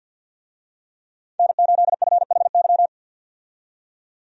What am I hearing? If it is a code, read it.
Morse code, heard as a keyed tone, about 48 wpm, 700 Hz. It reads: N8FHY
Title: USA Call Signs - 5 Characters - 50wpm